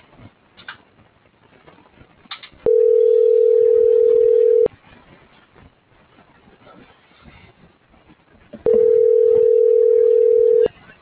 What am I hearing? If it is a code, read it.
Ringback tone (North America), 440 + 480 Hz.